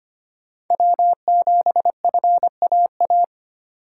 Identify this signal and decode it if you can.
Morse code, heard as a keyed tone, about 25 wpm, 700 Hz. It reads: W7FAA
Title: USA Call Signs - 5 Characters - 3x Character Spacing - 25wpm